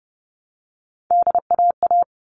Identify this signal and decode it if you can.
Morse code, heard as a keyed tone, about 30 wpm, 700 Hz. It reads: DAA
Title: All Cut Numbers - 3-Digit 30wpm